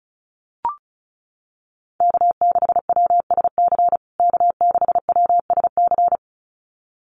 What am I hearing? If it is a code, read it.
Morse code, heard as a keyed tone, about 34 wpm, 700 Hz. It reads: K6WSC K6WSC
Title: USA Call Signs Repeated 2x 35wpm